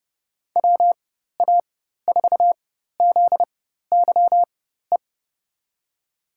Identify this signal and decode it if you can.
Morse code, heard as a keyed tone, about 30 wpm, spaced as for 14 wpm, 700 Hz. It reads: WA4ZYE
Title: USA Call Signs - 4x Character Spacing - 30wpm